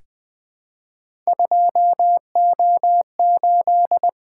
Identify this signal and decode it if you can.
Morse code, heard as a keyed tone, about 20 wpm, 700 Hz. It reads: 2O8
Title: Three Letter-Number Combinations - 20wpm